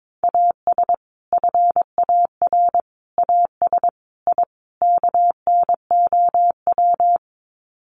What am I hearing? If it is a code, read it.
Morse code, heard as a keyed tone, about 22 wpm, 700 Hz. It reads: AS FAR AS I KNOW
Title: Sets of 5 Words 22wpm